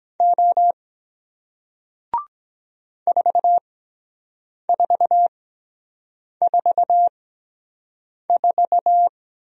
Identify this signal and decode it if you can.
Morse code, heard as a keyed tone, about 23 wpm, spaced as for 7 wpm, 700 Hz. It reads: O 4444